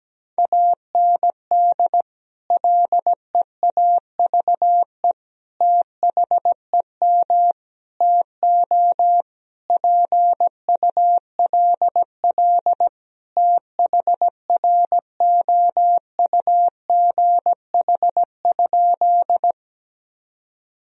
Morse code, heard as a keyed tone, about 17 wpm, 700 Hz. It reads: AND LEAVE THEM TO PULL THROUGH?
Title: Sentences from Top 1000 Words - Review Part 2 - 17wpm